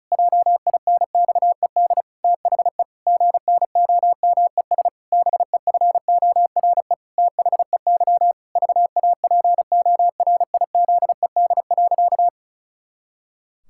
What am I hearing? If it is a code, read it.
Morse code, heard as a keyed tone, about 34 wpm, 700 Hz. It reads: JINXED THE GNOMES BEFORE THEY VAPORIZED.
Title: Pangram Challenge v2 - 35wpm